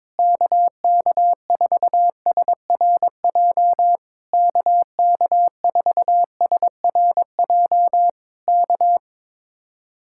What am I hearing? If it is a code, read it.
Morse code, heard as a keyed tone, about 22 wpm, 700 Hz. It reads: KK4SRJ KK4SRJ K
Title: Speed-Racing Common QSO Phrases Encore - 22wpm